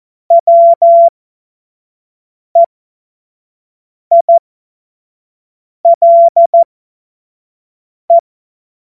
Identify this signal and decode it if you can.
Morse code, heard as a keyed tone, about 14 wpm, spaced as for 5 wpm, 680 Hz. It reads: WEILE
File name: Morsekurs / Witze / Witz 14. September 2021, 25 F70